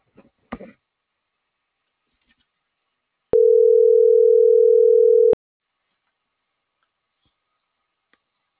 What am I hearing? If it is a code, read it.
Ringback tone (North America), 440 + 480 Hz.